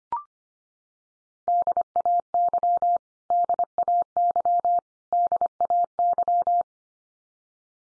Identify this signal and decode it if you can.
Morse code, heard as a keyed tone, about 25 wpm, 700 Hz. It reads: DAY DAY DAY
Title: Top 100 Words 25wpm x3